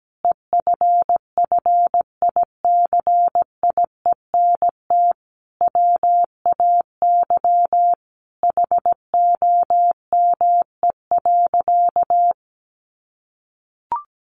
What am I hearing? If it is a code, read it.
Morse code, heard as a keyed tone, about 17 wpm, 700 Hz. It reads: EFFICIENT WAY HOME.
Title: Sentences from Top 1300 Words 17wpm